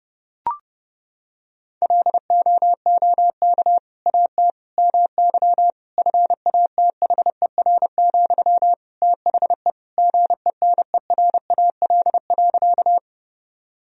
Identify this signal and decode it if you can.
Morse code, heard as a keyed tone, about 30 wpm, 700 Hz. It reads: LOOK AT MY FATHER, THE GENERAL.
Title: Sentences from Top 1000 Words - Review Part 3 - 30wpm